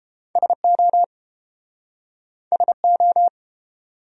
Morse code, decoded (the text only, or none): SO SO